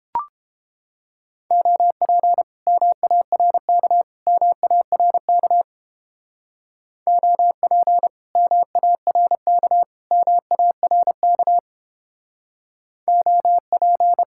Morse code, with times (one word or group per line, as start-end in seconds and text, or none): 1.5-2.4 OP
2.7-4.0 MARK
4.3-5.6 MARK
7.1-8.1 OP
8.4-9.8 MARK
10.1-11.6 MARK
13.1-14.2 OP